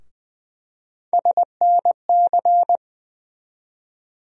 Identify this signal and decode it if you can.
Morse code, heard as a keyed tone, about 20 wpm, 700 Hz. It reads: SNC
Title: USA Call Sign Postfixes - 3 Letters 20wpm